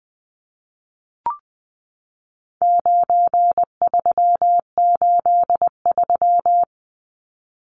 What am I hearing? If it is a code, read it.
Morse code, heard as a keyed tone, about 20 wpm, 700 Hz. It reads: 9383
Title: Numbers - 4-Digit 20wpm